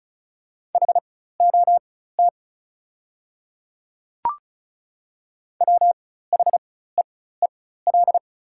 Morse code, heard as a keyed tone, about 36 wpm, spaced as for 16 wpm, 700 Hz. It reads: HOT WHEEL
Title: 4x Character Spacing - Top 500 Words - 35wpm